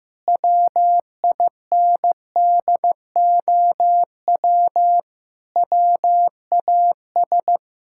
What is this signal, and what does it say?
Morse code, heard as a keyed tone, about 15 wpm, 700 Hz. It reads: WINDOW WAS
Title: Sentences from Top 1600 Words 15wpm